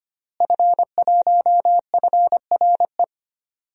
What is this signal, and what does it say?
Morse code, heard as a keyed tone, about 25 wpm, 700 Hz. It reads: F1FRE
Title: DXCC-100 Call Signs - 5 Characters - Repeated 3x - 25wpm